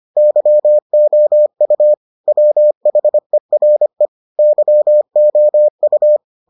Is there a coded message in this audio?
Yes: Morse code from 0.2 to 6.2 s, heard as a keyed tone, about 25 wpm, 600 Hz. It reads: YOU WHERE YOU